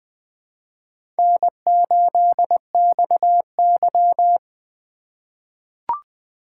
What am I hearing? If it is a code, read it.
Morse code, heard as a keyed tone, about 20 wpm, 700 Hz. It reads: N8XY
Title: USA Call Signs - 4 Characters - 3x Character Spacing - 20wpm